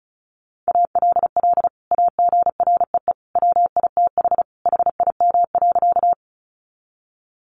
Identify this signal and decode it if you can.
Morse code, heard as a keyed tone, about 34 wpm, 700 Hz. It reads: ALL AGREE WITH HIM.
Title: Sentences from Top 1100 Words 35wpm